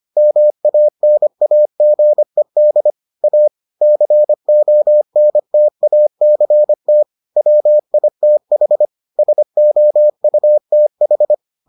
Morse code, heard as a keyed tone, about 25 wpm, 600 Hz. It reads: MANAGED A CONTACT WITH SOUTH